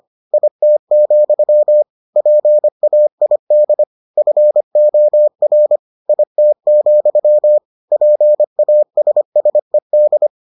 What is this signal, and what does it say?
Morse code, heard as a keyed tone, about 25 wpm, 600 Hz. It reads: IT, PAID FOR IT, PASSED